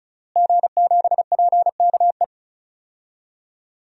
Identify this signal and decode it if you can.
Morse code, heard as a keyed tone, about 34 wpm, 700 Hz. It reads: G7PKE
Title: DXCC-100 Call Signs - 5 Characters - Repeated 3x - 35wpm